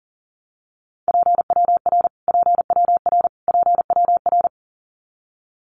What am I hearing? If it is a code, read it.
Morse code, heard as a keyed tone, about 40 wpm, 700 Hz. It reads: PWR PWR PWR
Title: Abbreviations Repeated 3x - 40wpm